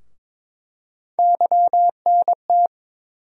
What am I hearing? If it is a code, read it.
Morse code, heard as a keyed tone, about 22 wpm, 700 Hz. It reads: YNT
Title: USA Call Sign Postfixes - 3 Letters 22wpm